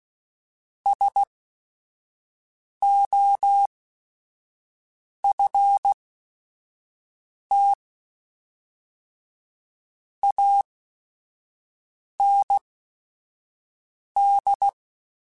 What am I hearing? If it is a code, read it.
Morse code, heard as a keyed tone, about 16 wpm, spaced as for 5 wpm, 800 Hz. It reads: SOFT AND